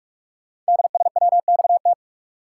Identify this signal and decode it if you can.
Morse code, heard as a keyed tone, about 44 wpm, 700 Hz. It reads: DSWXT
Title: Code Groups - 5 Letters - 45wpm